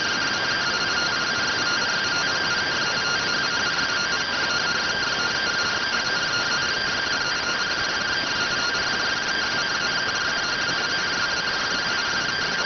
Two-tone teleprinter signal, about 170 Hz shift.
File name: findings / CT CORONAL 710mm 10mm scout.gen-35 rx